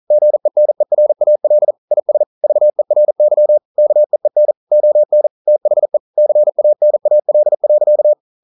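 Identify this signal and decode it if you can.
Morse code, heard as a keyed tone, about 42 wpm, 600 Hz. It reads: GENERAL IS VERY KEEN ON THE CANAL.
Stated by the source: Morse code at 40 wpm, practice text, CW med Gnister.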